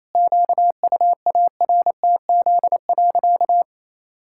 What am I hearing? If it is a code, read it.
Morse code, heard as a keyed tone, about 28 wpm, 700 Hz. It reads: QUARTZ.
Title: Pangram Challenge v2 - 28wpm